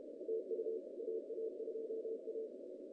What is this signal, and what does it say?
Morse code, heard as a keyed tone, about 33 wpm, 470 Hz. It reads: TU AA2A
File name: 2015 cq ww cw / 2015 cqwww k5tr - 0-47